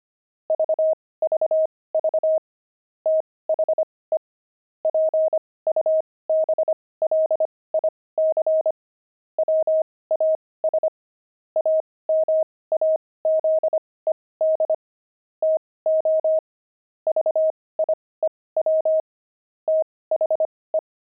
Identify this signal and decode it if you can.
Morse code, heard as a keyed tone, about 25 wpm, spaced as for 18 wpm, 630 Hz. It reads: VVV THE PUBLIC WAS AMAZED TO VIEW THE
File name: pangram-fs-18-cs-25-dexterity